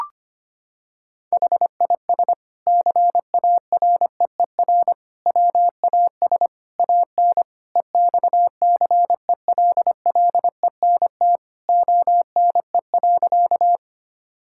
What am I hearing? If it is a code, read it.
Morse code, heard as a keyed tone, about 25 wpm, 700 Hz. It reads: HIS CAREER WAS AN EXCELLENT ONE.
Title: Sentences from Top 1000 Words - Review Part 6 - 25wpm